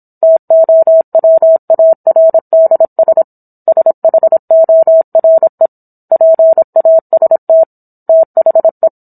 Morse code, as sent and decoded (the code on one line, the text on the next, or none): - --- .-- .- .-. -.. ... / ... .... --- .-. / .--. .- ... - / - .... .
TOWARDS SHORE PAST THE